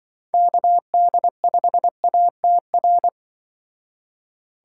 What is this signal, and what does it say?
Morse code, heard as a keyed tone, about 24 wpm, 700 Hz. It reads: KD5ATR
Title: Speed Racing - USA 2x3 Call Signs 20wpm